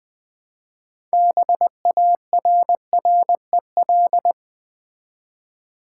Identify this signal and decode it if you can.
Morse code, heard as a keyed tone, about 20 wpm, 690 Hz. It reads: BARREL
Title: Build A Word - 2 Letters At A Time - 6-Letter Words - 20wpm